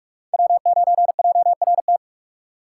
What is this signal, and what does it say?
Morse code, heard as a keyed tone, about 45 wpm, 700 Hz. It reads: W9JRT
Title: USA Call Signs - 5 Characters - 6x Character Spacing - 45wpm